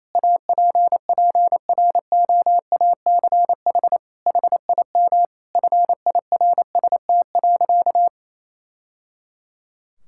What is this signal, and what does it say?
Morse code, heard as a keyed tone, about 28 wpm, 700 Hz. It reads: APPROACH HIM FIRST.
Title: Sentences from Top 1000 Words - Review Part 6 - 28wpm